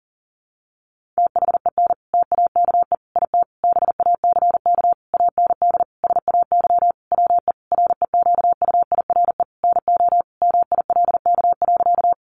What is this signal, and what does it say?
Morse code, heard as a keyed tone, about 40 wpm, 700 Hz. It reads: THEN TAKE IT BACK AND SAY WE REQUIRE NO MILK.
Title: Sentences from Top 1500 Words 40wpm